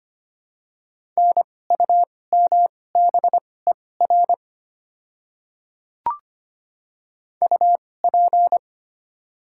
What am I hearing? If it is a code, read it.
Morse code, heard as a keyed tone, about 25 wpm, spaced as for 18 wpm, 700 Hz. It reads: NUMBER UP
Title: 2x Character Spacing - Top 100 Words - 25wpm